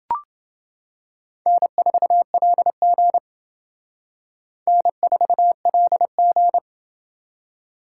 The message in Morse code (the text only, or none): N4LG N4LG